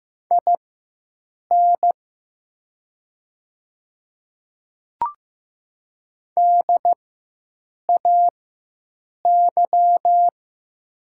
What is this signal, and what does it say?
Morse code, heard as a keyed tone, about 15 wpm, spaced as for 7 wpm, 700 Hz. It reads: IN DAY
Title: 4x Character Spacing - Top 100 Words - 15wpm